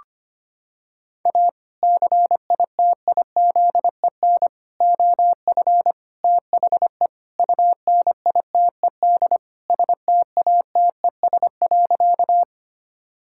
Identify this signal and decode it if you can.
Morse code, heard as a keyed tone, about 25 wpm, 700 Hz. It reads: A CITIZEN OF THE UNITED STATES.
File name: practice / 25wpm / Sentences with 3-Word Phrases - 25wpm